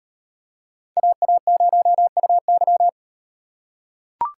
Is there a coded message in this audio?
Yes: Morse code from 1.0 to 2.9 s, heard as a keyed tone, about 38 wpm, 700 Hz. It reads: AA0UY